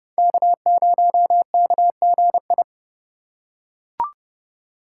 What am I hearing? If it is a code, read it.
Morse code, heard as a keyed tone, about 30 wpm, 700 Hz. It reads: K0KGI